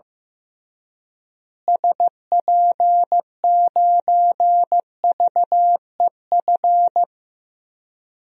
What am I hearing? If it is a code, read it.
Morse code, heard as a keyed tone, about 16 wpm, 700 Hz. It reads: SP9VEF
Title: DXCC-100 Call Signs - 6 Characters - Repeated 2x - 15wpm